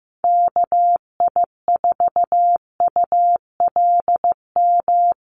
Morse code, heard as a keyed tone, about 14 wpm, 700 Hz. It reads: KI4ULM
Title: USA Call Signs Repeated 3x 15wpm